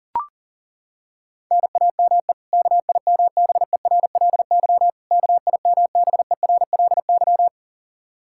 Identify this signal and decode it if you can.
Morse code, heard as a keyed tone, about 40 wpm, 700 Hz. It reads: NAME KIMBERLY KIMBERLY
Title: Practice QSO - Exchange 1 - Beginning - 40wpm